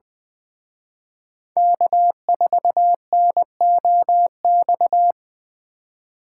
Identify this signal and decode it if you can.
Morse code, heard as a keyed tone, about 20 wpm, 700 Hz. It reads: K4NOX
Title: Speed Racing - USA Call Signs 17wpm